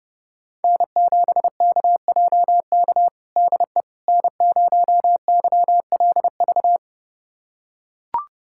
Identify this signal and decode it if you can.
Morse code, heard as a keyed tone, about 32 wpm, 700 Hz. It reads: N7KJK DE N0YLV
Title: USA Callsign DE Callsign - 5 Characters - 30wpm